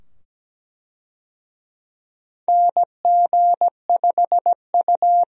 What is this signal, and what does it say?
Morse code, heard as a keyed tone, about 17 wpm, 700 Hz. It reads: NG5U
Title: USA Call Signs - 4 Characters - 17wpm